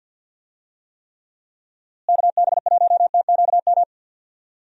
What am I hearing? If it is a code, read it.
Morse code, heard as a keyed tone, about 50 wpm, 700 Hz. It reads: KB1TQM
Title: USA Call Signs - 6 Characters - Repeated 3x - 50wpm